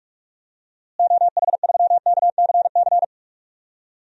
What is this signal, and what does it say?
Morse code, heard as a keyed tone, about 46 wpm, 700 Hz. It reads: OH3KCC
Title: DXCC-100 Call Signs - 6 Characters - Repeated 3x - 45wpm